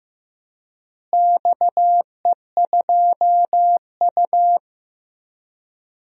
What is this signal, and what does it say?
Morse code, heard as a keyed tone, about 15 wpm, 700 Hz. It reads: XE2U
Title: DXCC-100 Call Signs - 4 Characters - Repeated 3x - 15wpm